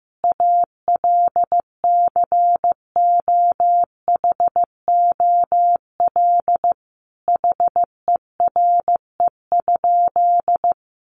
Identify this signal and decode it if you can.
Morse code, heard as a keyed tone, about 15 wpm, 700 Hz. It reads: ALCOHOL HERE?
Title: Sentences from Top 1400 Words 15wpm